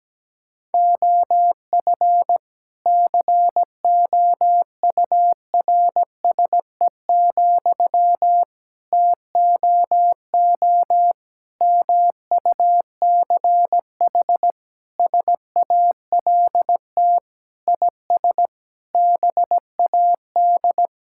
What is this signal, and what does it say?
Morse code, heard as a keyed tone, about 17 wpm, 700 Hz. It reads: OF COURSE, TOO MUCH SALT IS BAD
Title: Sentences from Top 1000 Words - Review Part 5 - 17wpm